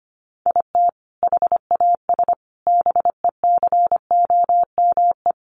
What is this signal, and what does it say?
Morse code, heard as a keyed tone, about 25 wpm, 700 Hz. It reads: IT HAS BECOME
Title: Sentences from Top 1200 Words 25wpm